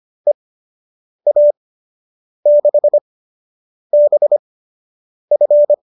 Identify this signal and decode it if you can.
Morse code, heard as a keyed tone, about 25 wpm, spaced as for 8 wpm, 590 Hz. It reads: EA6BF